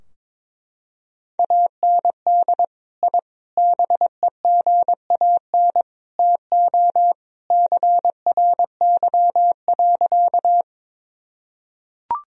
Morse code, heard as a keyed tone, about 22 wpm, 700 Hz. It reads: AND I BEGAN TO CRY.
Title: Sentences from Top 1000 Words - Review Part 2 - 22wpm